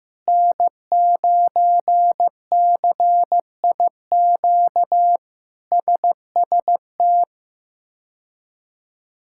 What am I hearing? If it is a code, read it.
Morse code, heard as a keyed tone, about 16 wpm, 700 Hz. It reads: N9CIQ SST
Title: K1USN SST QSO Practice - 15wpm